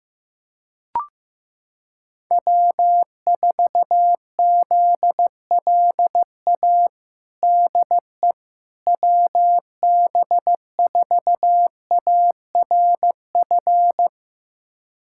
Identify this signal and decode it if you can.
Morse code, heard as a keyed tone, about 15 wpm, 700 Hz. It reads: W4ZLA DE WB4ARF
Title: USA Callsign DE Callsign 15wpm